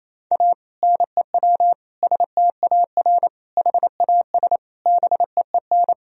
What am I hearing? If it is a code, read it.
Morse code, heard as a keyed tone, about 28 wpm, 700 Hz. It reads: A NEW STAR HAS BEEN
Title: Sentences from Top 1000 Words - Review Part 2 - 28wpm